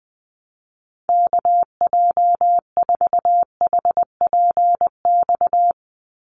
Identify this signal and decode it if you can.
Morse code, heard as a keyed tone, about 20 wpm, 700 Hz. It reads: KJ4HPX
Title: USA Call Signs 20wpm